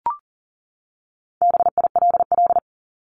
Morse code, heard as a keyed tone, about 40 wpm, 700 Hz. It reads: BILL
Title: Top 1501-1750 Words 40wpm